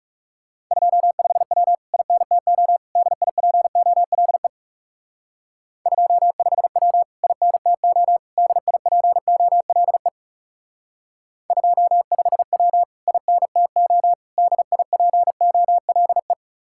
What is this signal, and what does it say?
Morse code, heard as a keyed tone, about 39 wpm, 700 Hz. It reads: 25W INTO DIPOLE 25W INTO DIPOLE 25W INTO DIPOLE